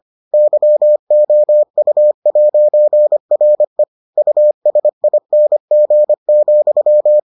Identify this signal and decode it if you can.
Morse code, heard as a keyed tone, about 25 wpm, 600 Hz. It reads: YOU'RE USING,